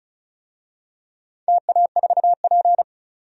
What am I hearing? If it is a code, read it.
Morse code, heard as a keyed tone, about 36 wpm, 700 Hz. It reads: TA4P
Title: DXCC-100 Call Signs - 4 Characters - 35wpm